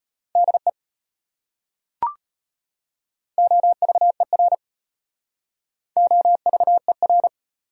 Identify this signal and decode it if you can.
Morse code, heard as a keyed tone, about 33 wpm, 700 Hz. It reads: DE OVER OVER